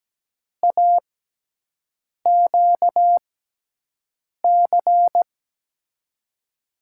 Morse code, heard as a keyed tone, about 17 wpm, spaced as for 6 wpm, 700 Hz. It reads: AQC
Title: USA Call Signs - 6 Characters - 6x Character Spacing - 17wpm